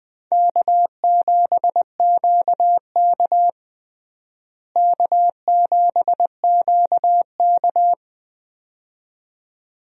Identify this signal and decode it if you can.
Morse code, heard as a keyed tone, about 21 wpm, 700 Hz. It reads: K7QK K7QK
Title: USA Call Signs - 4 Characters - Repeated 2x - 20wpm